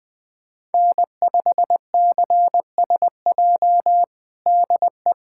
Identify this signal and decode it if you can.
Morse code, heard as a keyed tone, about 20 wpm, 700 Hz. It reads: N5CSJ DE
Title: USA Callsign DE Callsign - 5 Characters - 20wpm